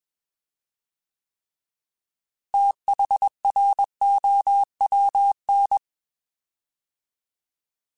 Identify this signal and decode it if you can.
Morse code, heard as a keyed tone, about 21 wpm, 800 Hz. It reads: THROWN
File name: TH / TH mp3s / Fast TH @20